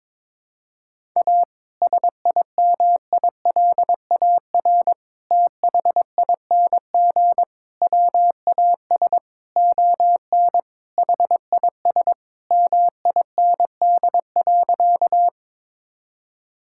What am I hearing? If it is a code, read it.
Morse code, heard as a keyed tone, about 22 wpm, 700 Hz. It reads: A SIMILAR THING WAS ON HIS MIND.
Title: Sentences from Top 1000 Words - Review Part 5 - 22wpm